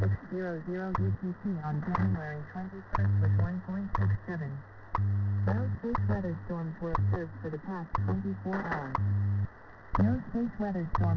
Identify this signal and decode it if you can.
Radio time signal, with speech.
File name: WebSDR records / WWVH 5000.000 am